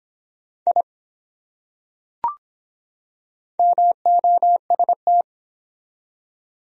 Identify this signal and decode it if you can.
Morse code, heard as a keyed tone, about 23 wpm, 700 Hz. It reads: I MOST